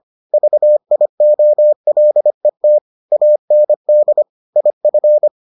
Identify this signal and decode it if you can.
Morse code, heard as a keyed tone, about 25 wpm, 600 Hz. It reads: VIOLET AND IF